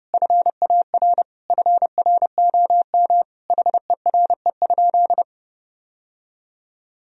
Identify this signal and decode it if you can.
Morse code, heard as a keyed tone, about 30 wpm, 700 Hz. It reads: FAR FROM HERE?
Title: Sentences from Top 1000 Words - Review Part 4 - 30wpm